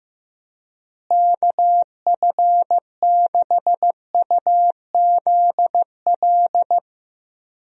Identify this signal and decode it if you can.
Morse code, heard as a keyed tone, about 16 wpm, 700 Hz. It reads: KF6UZL
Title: USA Call Signs - 6 Characters - Repeated 2x - 15wpm